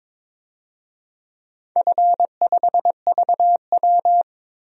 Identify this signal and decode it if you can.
Morse code, heard as a keyed tone, about 22 wpm, 700 Hz. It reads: F5VW
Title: DXCC-100 Call Signs - 4 Characters - 22wpm